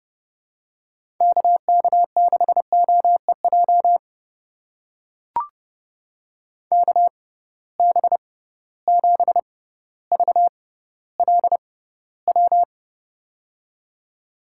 Morse code, heard as a keyed tone, about 30 wpm, spaced as for 15 wpm, 700 Hz. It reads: KK6OEJ KB7VLW
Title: USA Call Signs - 6 Characters - 6x Character Spacing - 30wpm